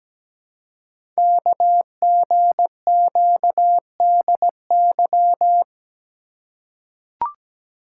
Morse code, heard as a keyed tone, about 17 wpm, 700 Hz. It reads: KGQDY
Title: Code Groups - 5 Letters - 17wpm